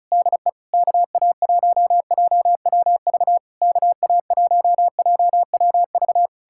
Morse code, heard as a keyed tone, about 38 wpm, 700 Hz. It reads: DE KA1JWV KA1JWV
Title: Speed-Racing Common QSO Phrases Encore - 30wpm